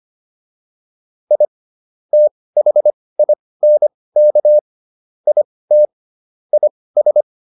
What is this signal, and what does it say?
Morse code, heard as a keyed tone, about 25 wpm, spaced as for 18 wpm, 600 Hz. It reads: I THINK IT IS